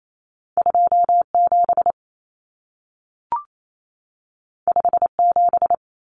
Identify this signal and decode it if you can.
Morse code, heard as a keyed tone, about 28 wpm, 700 Hz. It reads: 27 57
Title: Numbers - 2-Digit 28wpm